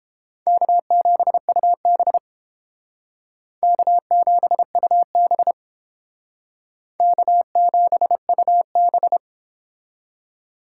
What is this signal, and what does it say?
Morse code, heard as a keyed tone, about 32 wpm, 700 Hz. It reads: K7UB K7UB K7UB